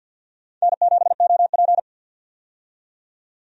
Morse code, heard as a keyed tone, about 51 wpm, 700 Hz. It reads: N7OP
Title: USA Call Signs - 4 Characters - 50wpm